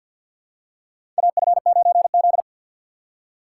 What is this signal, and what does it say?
Morse code, heard as a keyed tone, about 52 wpm, 700 Hz. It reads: AF9Z
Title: USA Call Signs - 4 Characters - 3x Character Spacing - 50wpm